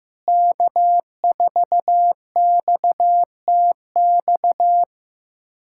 Morse code, heard as a keyed tone, about 15 wpm, 700 Hz. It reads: K4XTX
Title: USA Callsign DE Callsign - 5 Characters - 15wpm